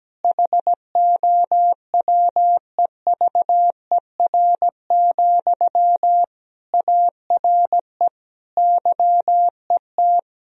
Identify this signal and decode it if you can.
Morse code, heard as a keyed tone, about 17 wpm, 700 Hz. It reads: HOWEVER, ARE YET
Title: Sentences from Top 1000 Words - Review Part 1 - 17wpm